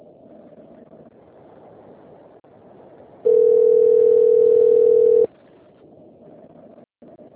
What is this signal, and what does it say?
Ringback tone (North America), 440 + 480 Hz.